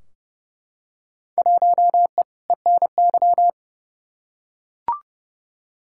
Morse code, heard as a keyed tone, about 30 wpm, 700 Hz. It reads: 1E ENY
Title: Field Day Exchanges - 30wpm